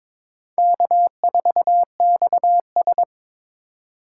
Morse code, heard as a keyed tone, about 26 wpm, 700 Hz. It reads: K4XS
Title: Speed Racing - USA 1x2 Call Signs 22wpm